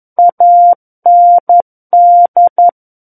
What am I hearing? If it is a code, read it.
Morse code, heard as a keyed tone, about 11 wpm, 700 Hz. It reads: AND
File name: 11 WPM